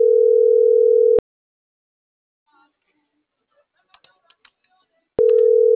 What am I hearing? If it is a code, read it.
Ringback tone (North America), 440 + 480 Hz.